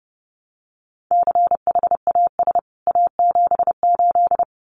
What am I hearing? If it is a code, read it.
Morse code, heard as a keyed tone, about 30 wpm, 700 Hz. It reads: CHAS A78
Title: CWT Exchange 30wpm